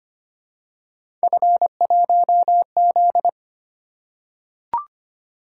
Morse code, heard as a keyed tone, about 25 wpm, 700 Hz. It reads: F1Z
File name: Three Letter-Number Combinations - 25wpm